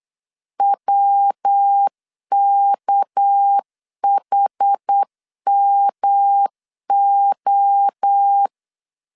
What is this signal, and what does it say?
Morse code, heard as a keyed tone, about 8 wpm, 800 Hz. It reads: WKHMO
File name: LEZIONI / 6 PKX0 / 6 PK 8wpm 3003